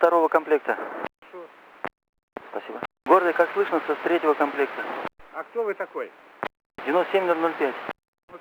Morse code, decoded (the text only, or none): none